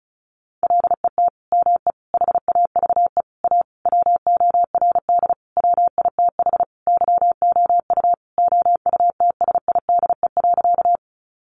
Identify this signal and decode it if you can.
Morse code, heard as a keyed tone, about 34 wpm, 700 Hz. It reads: LET ME HAVE A WORD WITH YOU OUTSIDE.
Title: Sentences from Top 900 Words 35wpm